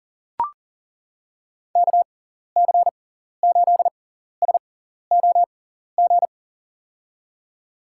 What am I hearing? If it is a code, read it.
Morse code, heard as a keyed tone, about 40 wpm, spaced as for 14 wpm, 700 Hz. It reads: KC8SOG
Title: USA Call Signs - 6 Characters - 6x Character Spacing - 40wpm